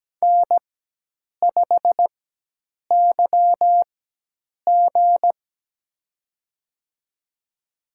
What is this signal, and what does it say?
Morse code, heard as a keyed tone, about 17 wpm, spaced as for 8 wpm, 700 Hz. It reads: N5YG
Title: USA Call Signs - 4 Characters - 4x Character Spacing - 17wpm